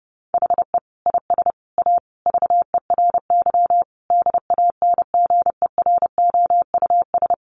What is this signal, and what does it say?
Morse code, heard as a keyed tone, about 30 wpm, 700 Hz. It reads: HE IS A VERY DANGEROUS
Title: Sentences from Top 1200 Words 30wpm